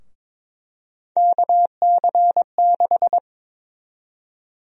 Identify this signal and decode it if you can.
Morse code, heard as a keyed tone, about 22 wpm, 700 Hz. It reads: KC6
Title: USA Call Sign Prefixes - Repeated 2x - 22wpm